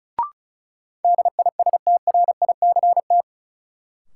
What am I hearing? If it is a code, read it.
Morse code, heard as a keyed tone, about 36 wpm, 700 Hz. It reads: DISTRICT
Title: Sentences from Top 1600 Words - New Word Drill Alt - 35wpm